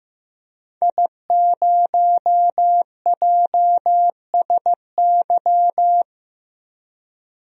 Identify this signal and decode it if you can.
Morse code, heard as a keyed tone, about 15 wpm, 700 Hz. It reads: I0JSY
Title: DXCC-100 Call Signs - 5 Characters - Repeated 3x - 15wpm